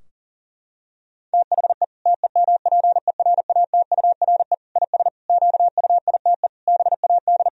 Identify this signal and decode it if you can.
Morse code, heard as a keyed tone, about 40 wpm, 700 Hz. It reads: THE TEMPERATURE IS QUITE BAD
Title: Sentences from Top 1000 Words - Review Part 4 - 40wpm